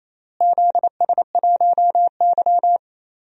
Morse code, heard as a keyed tone, about 28 wpm, 700 Hz. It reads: ZS1Y
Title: DXCC-100 Call Signs - 4 Characters - Repeated 2x - 28wpm